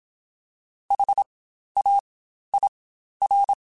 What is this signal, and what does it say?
Morse code, heard as a keyed tone, about 27 wpm, spaced as for 12 wpm, 800 Hz. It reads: HAIR
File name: Spaced Alphas